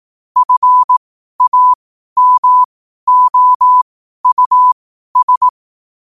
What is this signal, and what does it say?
Morse code, heard as a keyed tone, about 18 wpm, spaced as for 12 wpm, 1000 Hz. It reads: FAMOUS
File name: qotdfiles / 12wpm